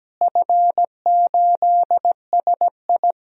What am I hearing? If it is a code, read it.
Morse code, heard as a keyed tone, about 17 wpm, 700 Hz. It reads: F8SI
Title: DXCC-100 Call Signs - 4 Characters - Repeated 2x - 17wpm